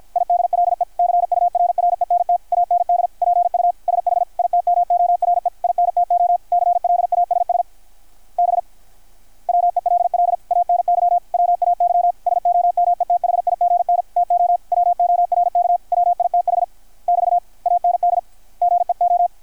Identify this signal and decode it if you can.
Morse code, heard as a keyed tone, about 53 wpm, 720 Hz, with radio noise. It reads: IDLE CURRENT AND PUSH IT MORE INTO CLASS B WELL ANY WAY SOMETHIGN TO WORK WITH <BT> AND GEO
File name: archived / CRSnet